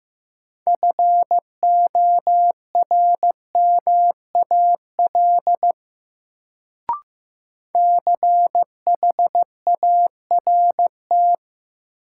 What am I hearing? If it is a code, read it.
Morse code, heard as a keyed tone, about 15 wpm, 700 Hz. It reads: FORMAL CHART
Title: Sentences from Top 1300 Words - New Word Drill Alt - 15wpm